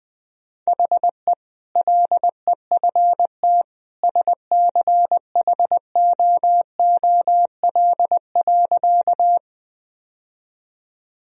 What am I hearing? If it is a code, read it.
Morse code, heard as a keyed tone, about 20 wpm, 700 Hz. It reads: HE LEFT SCHOOL.
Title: Sentences from Top 1000 Words - Review Part 1 - 20wpm